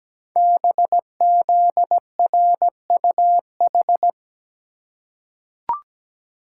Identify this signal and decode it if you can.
Morse code, heard as a keyed tone, about 17 wpm, 700 Hz. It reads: BZRUH
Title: Code Groups - 5 Letters - 17wpm